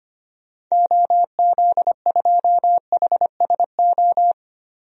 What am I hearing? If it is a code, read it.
Morse code, heard as a keyed tone, about 25 wpm, 700 Hz. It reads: OZ2HSO
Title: DXCC-100 Call Signs - 6 Characters - 25wpm